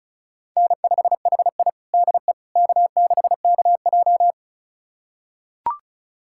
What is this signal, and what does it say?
Morse code, heard as a keyed tone, about 36 wpm, 700 Hz. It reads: N5HI DE K6KJ
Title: USA Callsign DE Callsign - 4 Characters - 35wpm